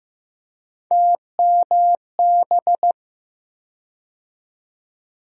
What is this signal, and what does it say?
Morse code, heard as a keyed tone, about 15 wpm, 700 Hz. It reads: TMB
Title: USA Call Sign Postfixes - 3 Letters 15wpm